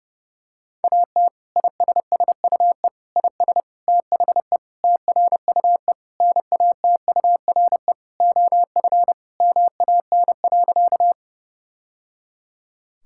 Morse code, heard as a keyed tone, about 30 wpm, 700 Hz. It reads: AT ISSUE IS THE TRUE NATURE OF MAN.
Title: Sentences from Top 1000 Words - Review Part 6 - 30wpm